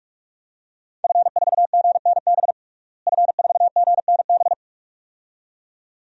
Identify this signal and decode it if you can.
Morse code, heard as a keyed tone, about 44 wpm, 700 Hz. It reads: F4GNB F4GNB
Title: DXCC-100 Call Signs - 5 Characters - Repeated 2x - 45wpm